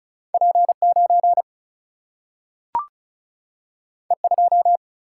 Morse code, heard as a keyed tone, about 36 wpm, 700 Hz. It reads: P 9 E 2
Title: Single-Letter Single-Number 35wpm